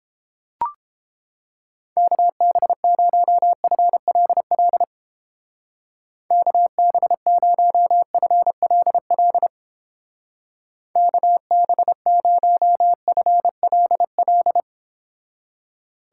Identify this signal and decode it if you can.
Morse code, heard as a keyed tone, about 29 wpm, 700 Hz. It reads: KB0FLL KB0FLL KB0FLL